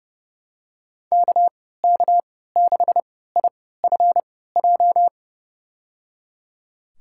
Morse code, heard as a keyed tone, about 30 wpm, spaced as for 17 wpm, 700 Hz. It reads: KK6IFJ
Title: USA Call Signs - 6 Characters - 3x Character Spacing - 30wpm